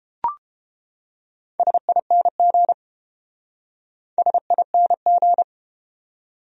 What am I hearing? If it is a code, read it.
Morse code, heard as a keyed tone, about 30 wpm, 700 Hz. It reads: SING SING